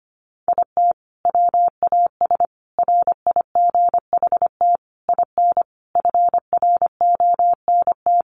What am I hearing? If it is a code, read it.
Morse code, heard as a keyed tone, about 25 wpm, 700 Hz. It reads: IT WAS RIGHT IN FRONT
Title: Sentences from Top 500 Words 25wpm